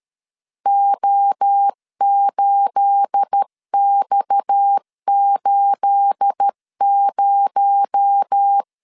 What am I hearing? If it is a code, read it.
Morse code, heard as a keyed tone, about 13 wpm, 800 Hz. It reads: O8X80
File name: LEZIONI / 9 28 12wpm 2351